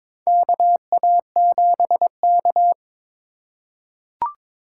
Morse code, heard as a keyed tone, about 21 wpm, 700 Hz. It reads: KA7K
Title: USA Call Signs - 4 Characters - 4x Character Spacing - 22wpm